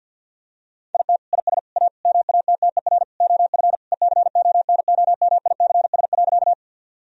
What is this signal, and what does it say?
Morse code, heard as a keyed tone, about 46 wpm, 700 Hz. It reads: IT IS A MATTER OF ECONOMICS.
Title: Sentences from Top 1000 Words - Review Part 3 - 50wpm